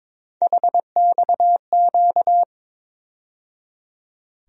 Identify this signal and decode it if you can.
Morse code, heard as a keyed tone, about 22 wpm, 700 Hz. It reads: HXQ
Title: USA Call Sign Postfixes - 3 Letters 22wpm